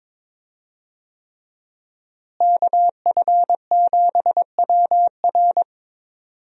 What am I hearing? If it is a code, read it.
Morse code, heard as a keyed tone, about 22 wpm, 700 Hz. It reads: KF7WR
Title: USA Call Signs - 5 Characters - Repeated 3x - 22wpm